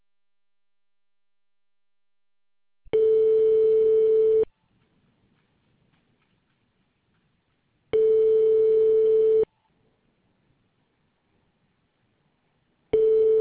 Ringback tone (France), 440 Hz.